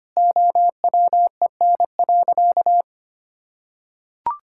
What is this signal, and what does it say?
Morse code, heard as a keyed tone, about 25 wpm, 700 Hz. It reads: OWEN.
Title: Pangram Challenge v2 - 25wpm